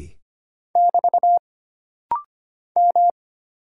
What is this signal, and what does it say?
Morse code, heard as a keyed tone, about 25 wpm, 700 Hz. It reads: <BT> M
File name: Single Character (Letter:Number:Punctuation:Prosign) - Rapid-Fire - 25wpm